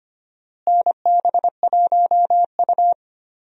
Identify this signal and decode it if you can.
Morse code, heard as a keyed tone, about 25 wpm, 700 Hz. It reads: NB1U
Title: USA Call Signs - 4 Characters - Repeated 2x - 25wpm